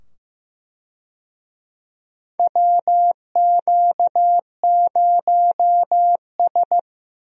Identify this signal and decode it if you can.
Morse code, heard as a keyed tone, about 15 wpm, 700 Hz. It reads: WQ0S
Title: USA Call Signs - 4 Characters - 15wpm